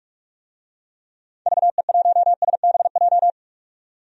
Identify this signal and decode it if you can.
Morse code, heard as a keyed tone, about 42 wpm, 700 Hz. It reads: VE1SBJ
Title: DXCC-100 Call Signs - 6 Characters - 45wpm